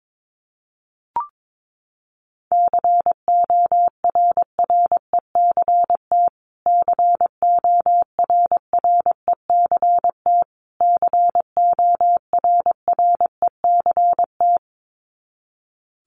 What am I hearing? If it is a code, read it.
Morse code, heard as a keyed tone, about 22 wpm, 700 Hz. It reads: CORRECT CORRECT CORRECT
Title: Top 500 Words Repeated 22wpm